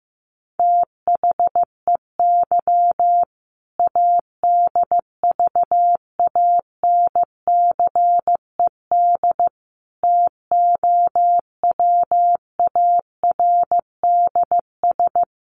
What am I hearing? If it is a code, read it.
Morse code, heard as a keyed tone, about 15 wpm, 700 Hz. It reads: THEY ADVANCED TOWARDS
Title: Sentences from Top 1200 Words 15wpm